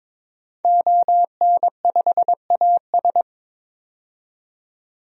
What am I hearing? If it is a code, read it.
Morse code, heard as a keyed tone, about 23 wpm, 700 Hz. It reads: ON5AS
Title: DXCC-100 Call Signs - 5 Characters - 22wpm